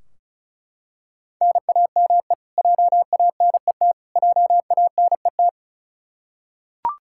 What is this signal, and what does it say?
Morse code, heard as a keyed tone, about 36 wpm, 700 Hz. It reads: NAME JANET JANET
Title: Practice QSO - Exchange 1 - Beginning - 35wpm